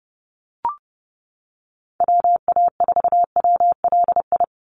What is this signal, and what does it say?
Morse code, heard as a keyed tone, about 30 wpm, 700 Hz. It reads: WA4WLI
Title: Call Signs - 30wpm